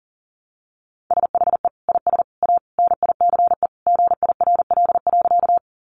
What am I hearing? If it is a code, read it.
Morse code, heard as a keyed tone, about 40 wpm, 700 Hz. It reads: SHE IS A NICE GIRL.
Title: Sentences from Top 1100 Words 40wpm